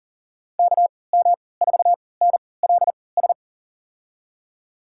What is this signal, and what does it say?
Morse code, heard as a keyed tone, about 40 wpm, spaced as for 23 wpm, 700 Hz. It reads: KM4NLS
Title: USA Call Signs - 6 Characters - 3x Character Spacing - 40wpm